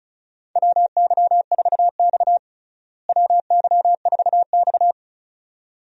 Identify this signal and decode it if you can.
Morse code, heard as a keyed tone, about 36 wpm, 700 Hz. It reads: WY4X WY4X
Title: USA Call Signs - 4 Characters - Repeated 2x - 35wpm